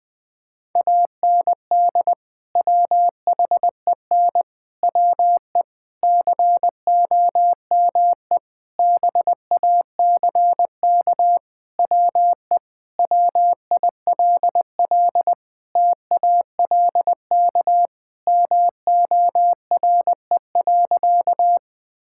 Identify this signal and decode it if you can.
Morse code, heard as a keyed tone, about 20 wpm, 700 Hz. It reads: AND WHEN WE COME BACK WE WILL TALK MORE.
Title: Sentences with 5-Word Phrases - Spoken Beginning - 20wpm